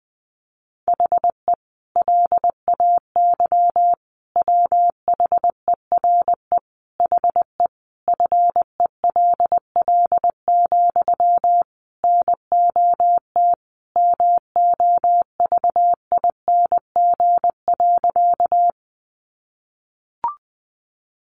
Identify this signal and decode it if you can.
Morse code, heard as a keyed tone, about 20 wpm, 700 Hz. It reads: HE LAY WHERE HE FELL, NOT MOVING.
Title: A Sound Of Thunder 20wpm